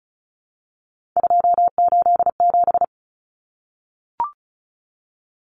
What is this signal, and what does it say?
Morse code, heard as a keyed tone, about 34 wpm, 700 Hz. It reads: 287
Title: Numbers - 3-Digit 35wpm